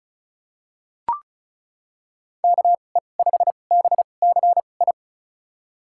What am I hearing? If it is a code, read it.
Morse code, heard as a keyed tone, about 34 wpm, spaced as for 25 wpm, 700 Hz. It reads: KE5BCI
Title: USA Call Signs - 6 Characters - 2x Character Spacing - 35wpm